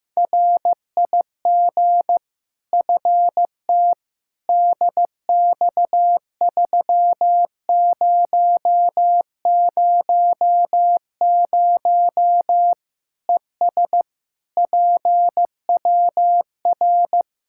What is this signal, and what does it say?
Morse code, heard as a keyed tone, about 15 wpm, 700 Hz. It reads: RIG FT DX3000 ES PWR
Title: Practice QSO - Exchange 2 - Info - 15wpm